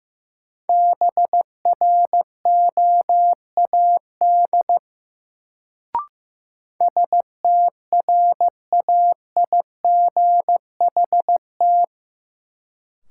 Morse code, heard as a keyed tone, about 15 wpm, 700 Hz. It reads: BROAD STRAIGHT
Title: Sentences from Top 1000 Words - New Word Drill - 15wpm